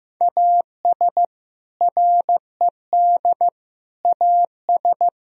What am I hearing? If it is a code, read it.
Morse code, heard as a keyed tone, about 15 wpm, 700 Hz. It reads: AS RED AS
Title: Sentences from Top 1000 Words - Review Part 2 - 15wpm